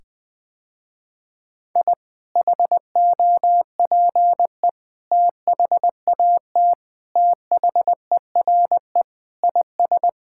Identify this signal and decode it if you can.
Morse code, heard as a keyed tone, about 21 wpm, 700 Hz. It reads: I HOPE THAT THERE IS
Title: Sentences with 2-Word Phrases - Spoken Beginning - 20wpm